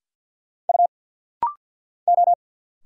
Morse code, heard as a keyed tone, about 50 wpm, 700 Hz. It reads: U O
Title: Single Letter-Number Alt - Rapid-Fire 50wpm